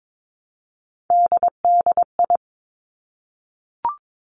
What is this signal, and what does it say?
Morse code, heard as a keyed tone, about 22 wpm, 700 Hz. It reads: DDI
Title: Three Challenging Letter-Number Combinations 22wpm